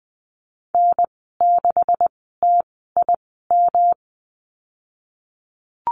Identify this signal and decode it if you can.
Morse code, heard as a keyed tone, about 19 wpm, spaced as for 14 wpm, 700 Hz. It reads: N6TIM
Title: 186 Lesson 30g - Letter b - Call Signs 20wpm